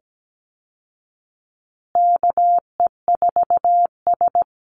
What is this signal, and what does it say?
Morse code, heard as a keyed tone, about 17 wpm, 700 Hz. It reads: KE4S
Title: Call Signs Repeated 3x - 17wpm